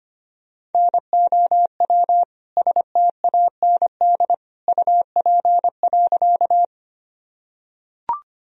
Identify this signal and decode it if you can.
Morse code, heard as a keyed tone, about 25 wpm, 700 Hz. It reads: NOW STAND UP.
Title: Sentences from Top 1000 Words - Review Part 1 - 25wpm